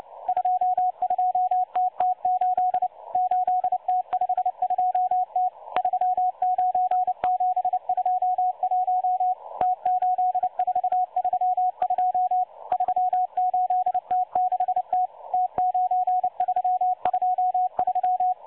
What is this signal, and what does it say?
Morse code, heard as a keyed tone, about 29 wpm, 730 Hz. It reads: 22TT8 8T52T 39721 T8432 38T6T T9323
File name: M12-Morse-Numbers-Station-8116-kHz-CW